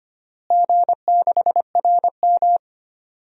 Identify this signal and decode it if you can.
Morse code, heard as a keyed tone, about 25 wpm, 700 Hz. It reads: G6RM
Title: DXCC-100 Call Signs - 4 Characters - Repeated 3x - 25wpm